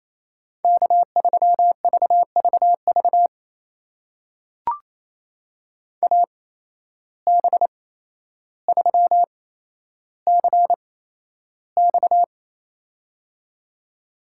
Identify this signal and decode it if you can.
Morse code, heard as a keyed tone, about 27 wpm, spaced as for 12 wpm, 700 Hz. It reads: K3VVV AB3CX
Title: USA Call Signs - 5 Characters - 8x Character Spacing - 28wpm